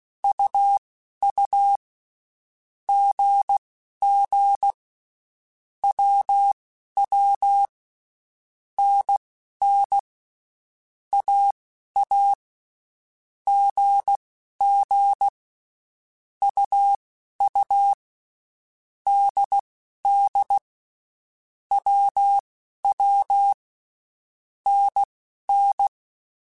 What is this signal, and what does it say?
Morse code, heard as a keyed tone, about 16 wpm, spaced as for 11 wpm, 800 Hz. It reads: UU GG WW NN AA GG UU DD WW NN
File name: Basic Letter Groups By7Groups / AWUNDG / AWUNDG@10